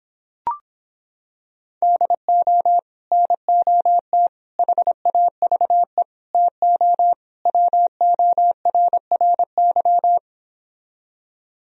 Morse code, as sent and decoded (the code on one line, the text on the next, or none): -.. --- / -. --- - / .... .- ...- . / - --- / .-- --- .-. .-. -.--
DO NOT HAVE TO WORRY